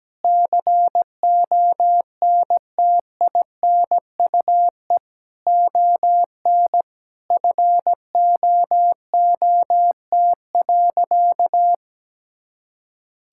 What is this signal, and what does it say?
Morse code, heard as a keyed tone, about 17 wpm, 700 Hz. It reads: CONTINUE ON FOOT.
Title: Sentences from Top 1000 Words - Review Part 6 - 17wpm